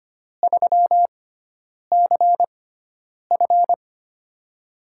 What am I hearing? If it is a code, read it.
Morse code, heard as a keyed tone, about 25 wpm, spaced as for 9 wpm, 700 Hz. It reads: 3CF
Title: USA Call Signs - 4 Characters - 6x Character Spacing - 25wpm